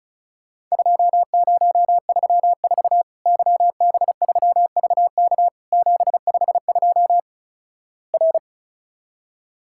Morse code, heard as a keyed tone, about 34 wpm, 700 Hz. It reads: 2034 YB3VK 752 R
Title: Worked All Europe DX Contest - QTC Exchange - 35wpm